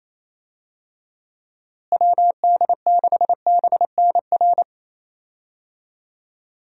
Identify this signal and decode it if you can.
Morse code, heard as a keyed tone, about 28 wpm, 700 Hz. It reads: WD6BNR
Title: USA Call Signs - 6 Characters - Repeated 2x - 28wpm